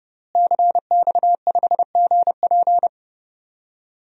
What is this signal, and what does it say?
Morse code, heard as a keyed tone, about 32 wpm, 700 Hz. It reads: CX5GP
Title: DXCC-100 Call Signs - 5 Characters - Repeated 2x - 30wpm